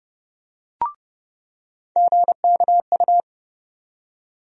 Morse code, heard as a keyed tone, about 31 wpm, 700 Hz. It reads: GKU
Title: USA Call Sign Postfixes - 3 Letters 30wpm